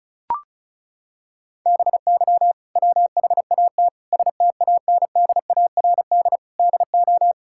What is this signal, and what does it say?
Morse code, heard as a keyed tone, about 36 wpm, 700 Hz. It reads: BY WHAT STANDARD DO
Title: Sentences from Top 1000 Words - Review Part 6 - 35wpm